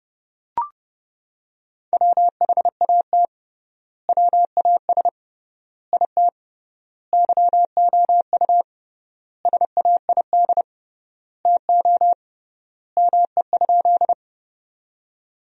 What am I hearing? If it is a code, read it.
Morse code, heard as a keyed tone, about 30 wpm, 700 Hz. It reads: WHAT WAS IT YOU SAID TO ME?
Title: 3x Word-Spacing - Sentences from Top 200 Words - 30wpm